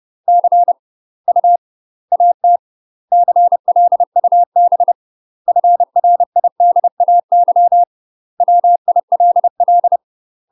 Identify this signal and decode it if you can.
Morse code, heard as a keyed tone, about 30 wpm, 700 Hz. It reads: C U AT CLUB FRIDAY WILL
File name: UK MASTER QSOs TRAINING / 30WPM / UK MASTER QSOs 0043